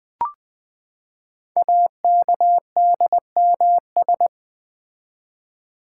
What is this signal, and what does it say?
Morse code, heard as a keyed tone, about 20 wpm, 700 Hz. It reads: AKDMS
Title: Code Groups - 5 Letters - 20wpm